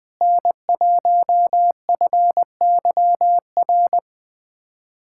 Morse code, heard as a keyed tone, about 20 wpm, 700 Hz. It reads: N1FYR
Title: USA Callsign DE Callsign 20wpm